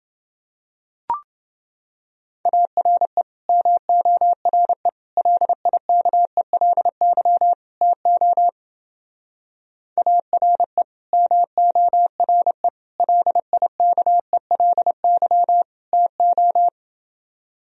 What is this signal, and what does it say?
Morse code, heard as a keyed tone, about 30 wpm, 700 Hz. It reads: ARE MORE LIKELY TO ARE MORE LIKELY TO